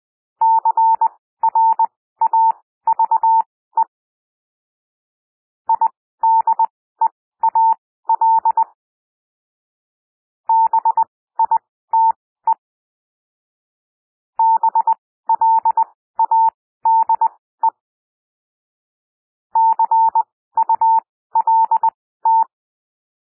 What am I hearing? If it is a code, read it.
Morse code, heard as a keyed tone, about 20 wpm, spaced as for 14 wpm, 900 Hz. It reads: CRAVE IDEAL BITE BLADE CULT
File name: SOUND WORDS 20 wpm